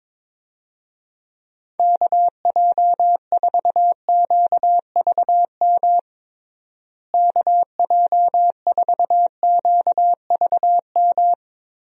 Morse code, heard as a keyed tone, about 22 wpm, 700 Hz. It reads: KJ4QVM KJ4QVM
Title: USA Call Signs - 6 Characters - Repeated 2x - 22wpm